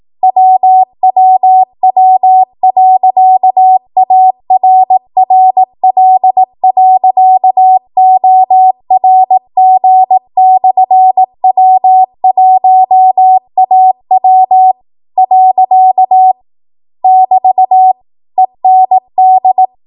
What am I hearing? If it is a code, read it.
Morse code, heard as a keyed tone, about 18 wpm, 750 Hz. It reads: WWW.ARRL.ORG/W1AW . <BT> END
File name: Archive / 18 WPM / 18WPM